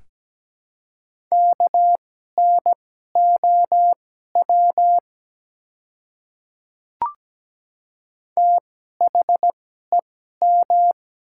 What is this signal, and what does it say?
Morse code, heard as a keyed tone, about 17 wpm, spaced as for 12 wpm, 690 Hz. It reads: KNOW THEM
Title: 2x Character Spacing - Top 100 Words - 17wpm